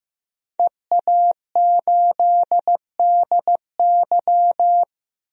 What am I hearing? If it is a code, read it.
Morse code, heard as a keyed tone, about 15 wpm, 700 Hz. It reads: EA8DY
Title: DXCC-100 Call Signs - 5 Characters - 15wpm